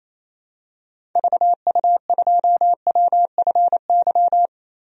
Morse code, heard as a keyed tone, about 28 wpm, 700 Hz. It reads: VU2WFY
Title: DXCC-100 Call Signs - 6 Characters - 28wpm